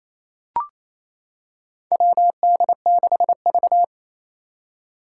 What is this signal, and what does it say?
Morse code, heard as a keyed tone, about 27 wpm, 700 Hz. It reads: WD6V
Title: Four Challenging Letter-Number Combinations 28wpm